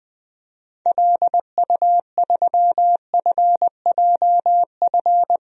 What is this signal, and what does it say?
Morse code, heard as a keyed tone, about 21 wpm, 700 Hz. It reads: LU3FJF
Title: DXCC-100 Call Signs - 6 Characters - Repeated 2x - 20wpm